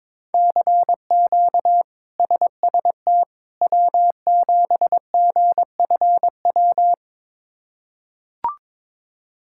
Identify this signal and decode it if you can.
Morse code, heard as a keyed tone, about 22 wpm, 700 Hz. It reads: CQ SST W7GFW
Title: K1USN SST QSO Practice - 22wpm